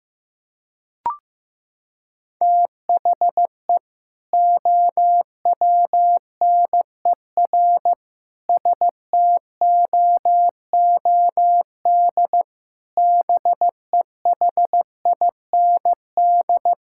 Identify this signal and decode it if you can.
Morse code, heard as a keyed tone, about 15 wpm, 700 Hz. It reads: THE OWNER STOOD BEHIND
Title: Sentences from Top 1600 Words 15wpm